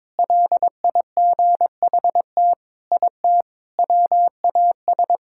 Morse code, heard as a keyed tone, about 22 wpm, 700 Hz. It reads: LIGHT IT WAS
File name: practice / 22wpm / Sentences with 4-Word Phrases - 22wpm